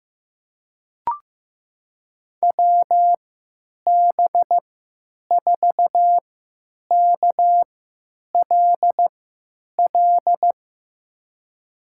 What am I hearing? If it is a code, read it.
Morse code, heard as a keyed tone, about 15 wpm, spaced as for 9 wpm, 700 Hz. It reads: WB4KLL
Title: USA Call Signs - 6 Characters - 3x Character Spacing - 15wpm